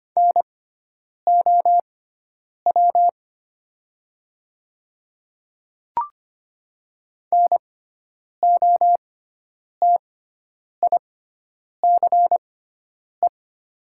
Morse code, heard as a keyed tone, about 25 wpm, spaced as for 9 wpm, 700 Hz. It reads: NOW NOTICE
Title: 6x Character Spacing - Top 500 Words - 25wpm